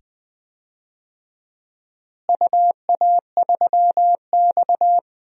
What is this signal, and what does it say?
Morse code, heard as a keyed tone, about 20 wpm, 700 Hz. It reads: UA3X
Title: DXCC-100 Call Signs - 4 Characters - Repeated 2x - 20wpm